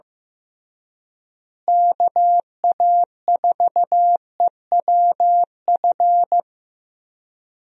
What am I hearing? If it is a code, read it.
Morse code, heard as a keyed tone, about 16 wpm, 700 Hz. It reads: KA4EWF
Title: USA Call Signs - 6 Characters - Repeated 3x - 15wpm